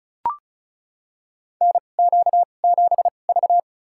Morse code, heard as a keyed tone, about 34 wpm, spaced as for 25 wpm, 690 Hz. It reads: NQ7V
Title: USA Call Signs - 4 Characters - 2x Character Spacing - 35wpm